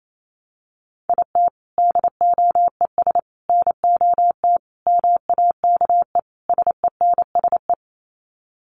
Morse code, heard as a keyed tone, about 28 wpm, 700 Hz. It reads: IT DOES NOT MAKE SENSE
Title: Sets of 5 Words Encore2 28wpm